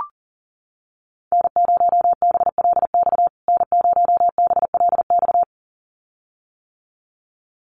Morse code, heard as a keyed tone, about 40 wpm, 700 Hz. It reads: N0BLX N0BLX
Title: USA Call Signs Repeated 2x 40wpm